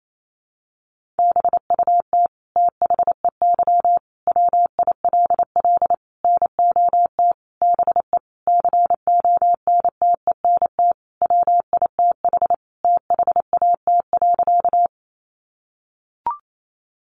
Morse code, heard as a keyed tone, about 28 wpm, 700 Hz. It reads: BUT THEY WILL NOT BE CONTENT WITH THAT.
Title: Sentences from Top 900 Words 28wpm